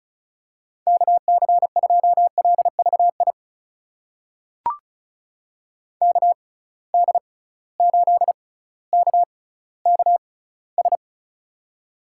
Morse code, heard as a keyed tone, about 34 wpm, spaced as for 18 wpm, 700 Hz. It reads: KC2LVI KD8KKS
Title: USA Call Signs - 6x Character Spacing - 35wpm